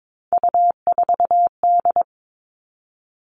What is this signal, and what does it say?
Morse code, heard as a keyed tone, about 22 wpm, 700 Hz. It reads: U4D
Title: All Cut Numbers - 3-Digit 22wpm